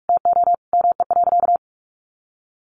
Morse code, heard as a keyed tone, about 44 wpm, 700 Hz. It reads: TO ME.
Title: Sentences from Top 200 Words 45wpm